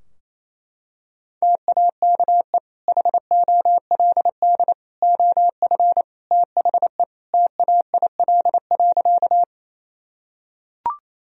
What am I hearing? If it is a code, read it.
Morse code, heard as a keyed tone, about 28 wpm, 700 Hz. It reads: TAKE HOLD OF THE TAIL.
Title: Sentences from Top 1000 Words - Review Part 3 - 28wpm